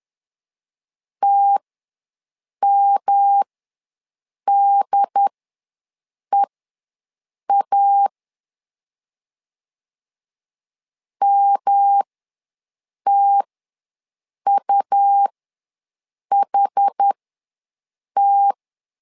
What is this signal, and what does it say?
Morse code, heard as a keyed tone, about 10 wpm, spaced as for 6 wpm, 800 Hz. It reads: TMDEA MTUHT
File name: LEZIONI / 3 MWRO / 3 M ws 10wpm 2121